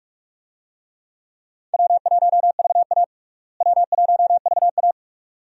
Morse code, heard as a keyed tone, about 46 wpm, 700 Hz. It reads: W1VA W1VA
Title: USA Call Signs - 4 Characters - Repeated 3x - 45wpm